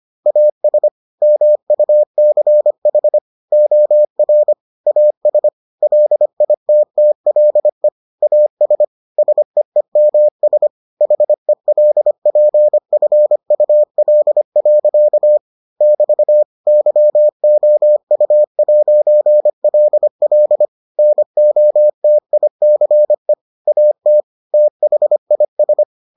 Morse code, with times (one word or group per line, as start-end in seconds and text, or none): 0.3-0.9 AS
1.2-3.2 MUCH
3.5-4.5 OR
4.9-5.5 AS
5.8-7.9 LITTLE
8.2-8.8 AS
9.2-10.7 SEEMS
11.0-15.4 HELPFUL.
15.8-16.4 <BT>
16.7-20.6 YOU'LL
21.0-23.3 NOTICE
23.7-24.2 AT
24.5-25.8 THIS